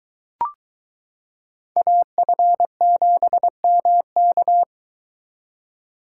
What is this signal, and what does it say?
Morse code, heard as a keyed tone, about 22 wpm, 700 Hz. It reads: AF7MK